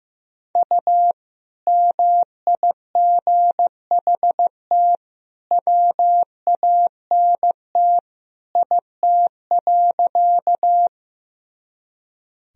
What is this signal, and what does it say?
Morse code, heard as a keyed tone, about 15 wpm, 700 Hz. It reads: U MIGHT WANT IT.
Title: Sentences from Top 300 Words and Abbreviations 15wpm